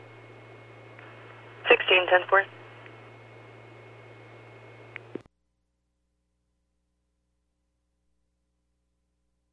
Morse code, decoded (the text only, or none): none